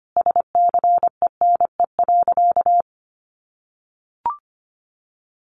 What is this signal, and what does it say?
Morse code, heard as a keyed tone, about 25 wpm, 700 Hz. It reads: SCENE.
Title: Sentences from Top 1100 Words 25wpm